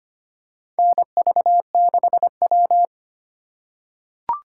Morse code, heard as a keyed tone, about 25 wpm, 700 Hz. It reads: NV6W
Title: USA Call Signs - 4 Characters - 3x Character Spacing - 25wpm